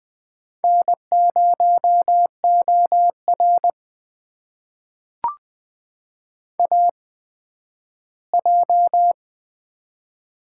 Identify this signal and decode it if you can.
Morse code, heard as a keyed tone, about 20 wpm, spaced as for 12 wpm, 700 Hz. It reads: N0OR AJ
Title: USA Call Signs - 4 Characters - 8x Character Spacing - 20wpm